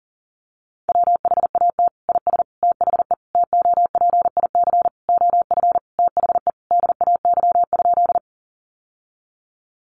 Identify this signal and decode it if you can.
Morse code, heard as a keyed tone, about 40 wpm, 700 Hz. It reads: WHAT IS THE TOPIC OF THE DAY?
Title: Sentences from Top 1200 Words 40wpm